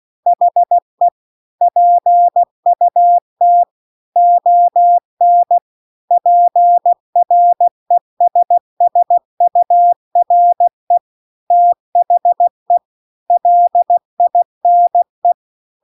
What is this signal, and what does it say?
Morse code, heard as a keyed tone, about 16 wpm, 700 Hz. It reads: HE PUT ON PRESSURE THE LINE
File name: CW-The-Big-Twoheaded-River-2-16WPM